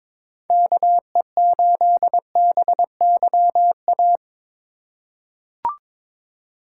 Morse code, heard as a keyed tone, about 22 wpm, 700 Hz. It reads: KE8BYA
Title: USA Call Signs - 6 Characters - 2x Character Spacing - 22wpm